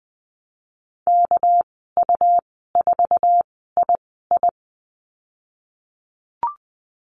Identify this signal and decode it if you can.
Morse code, heard as a keyed tone, about 20 wpm, spaced as for 15 wpm, 700 Hz. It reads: KU4II
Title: 200 Lesson 32g - Letter k - Call Signs 20wpm